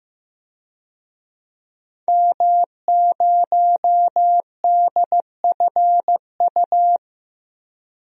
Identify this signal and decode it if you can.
Morse code, heard as a keyed tone, about 15 wpm, 700 Hz. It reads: M0DFU
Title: DXCC-100 Call Signs - 5 Characters - 15wpm